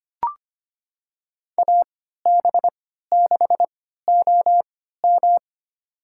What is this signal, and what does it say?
Morse code, heard as a keyed tone, about 25 wpm, spaced as for 14 wpm, 700 Hz. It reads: AB6OM
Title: USA Call Signs - 5 Characters - 3x Character Spacing - 25wpm